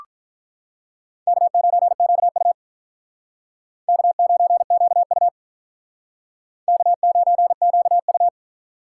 Morse code, heard as a keyed tone, about 45 wpm, 700 Hz. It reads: K9QU K9QU K9QU